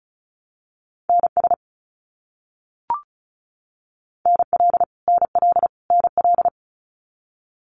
Morse code, heard as a keyed tone, about 34 wpm, 700 Hz. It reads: NS NL NL NL
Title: Canadian Provinces and Territories - Repeated 3x - 35wpm